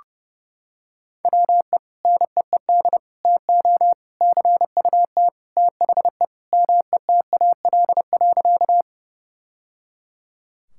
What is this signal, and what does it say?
Morse code, heard as a keyed tone, about 30 wpm, 700 Hz. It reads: WE NEED TO CUT THE METAL.
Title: Sentences from Top 1000 Words - Review Part 6 - 30wpm